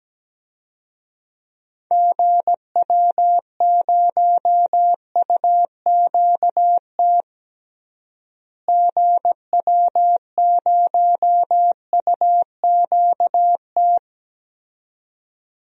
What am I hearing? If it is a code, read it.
Morse code, heard as a keyed tone, about 17 wpm, 700 Hz. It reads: GW0UQT GW0UQT
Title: DXCC-100 Call Signs - 6 Characters - Repeated 2x - 17wpm